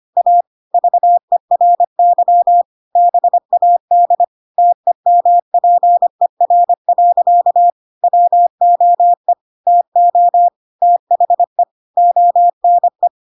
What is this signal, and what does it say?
Morse code, heard as a keyed tone, about 25 wpm, 700 Hz. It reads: A VERY BAD TEMPER. WOE TO THE ONE
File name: Pinocchio / Pinocchio ch2 25 25 ik1hge